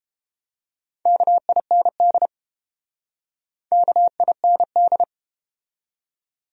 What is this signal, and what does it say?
Morse code, heard as a keyed tone, about 32 wpm, 700 Hz. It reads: KIND KIND